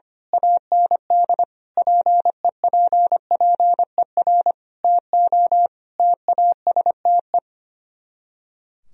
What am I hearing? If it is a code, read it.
Morse code, heard as a keyed tone, about 24 wpm, 700 Hz. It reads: AND PEPPER TO TASTE
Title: Speed-Racing - Sets of 5 Words 25wpm